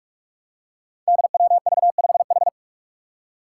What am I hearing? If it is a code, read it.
Morse code, heard as a keyed tone, about 42 wpm, 700 Hz. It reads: DWV5H
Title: Five Challenging Letter-Number Combinations 45wpm